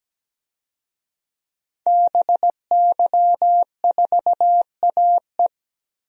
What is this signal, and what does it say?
Morse code, heard as a keyed tone, about 17 wpm, 700 Hz. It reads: BY4AE
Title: DXCC-100 Call Signs - 5 Characters - Repeated 2x - 17wpm